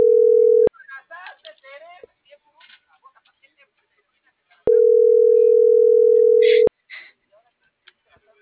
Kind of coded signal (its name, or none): telephone call progress tone